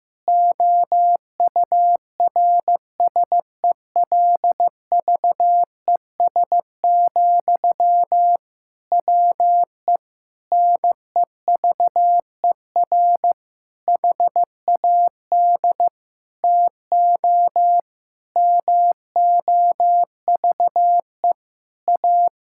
Morse code, heard as a keyed tone, about 15 wpm, 700 Hz. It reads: OURSELVES, WE NEVER HAD TO MOVE A
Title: Sentences from Top 1600 Words 15wpm